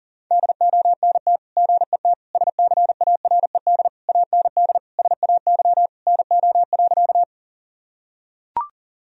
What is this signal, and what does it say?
Morse code, heard as a keyed tone, about 40 wpm, 700 Hz. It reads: DONT GET SCARED AND SAY NO.
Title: Sentences from Top 1000 Words - Review Part 4 - 40wpm